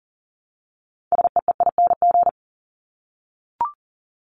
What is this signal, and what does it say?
Morse code, heard as a keyed tone, about 40 wpm, 700 Hz. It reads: SEEING
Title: ING Suffix - Top 100 Words with Base Words 40wpm